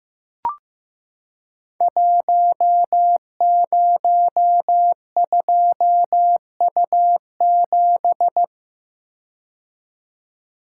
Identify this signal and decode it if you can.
Morse code, heard as a keyed tone, about 15 wpm, 700 Hz. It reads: 102U7
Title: Code Groups - 5 Letter:Numbers - 15wpm